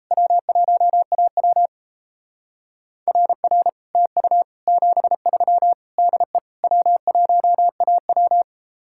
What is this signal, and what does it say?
Morse code, heard as a keyed tone, about 35 wpm, 700 Hz. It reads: W1AW RR TU 73 DE W1AW